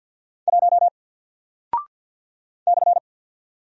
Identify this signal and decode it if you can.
Morse code, heard as a keyed tone, about 48 wpm, 700 Hz. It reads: 1 /
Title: Single Character (Letter:Number:Punctuation:Prosign) Alt 50wpm